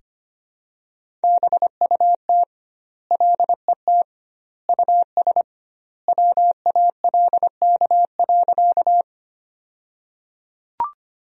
Morse code, heard as a keyed tone, about 25 wpm, 700 Hz. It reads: BUT LET US WALK.
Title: 2x Word-Spacing - Sentences from Top 300 Words - 25wpm